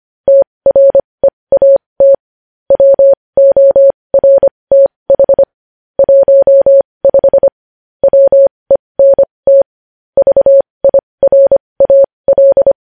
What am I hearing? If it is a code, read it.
Morse code, heard as a keyed tone, about 25 wpm, spaced as for 20 wpm, 550 Hz. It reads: TREAT WORTH 15 WENT VIRAL
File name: podcast / News headlines-1x-25x20